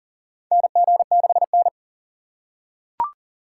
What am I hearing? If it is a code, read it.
Morse code, heard as a keyed tone, about 40 wpm, 700 Hz. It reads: NG6N
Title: USA Call Signs - 4 Characters - 4x Character Spacing - 40wpm